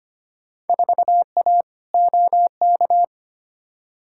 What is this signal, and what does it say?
Morse code, heard as a keyed tone, about 25 wpm, 700 Hz. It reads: 4A OK
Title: Field Day Exchanges - 25wpm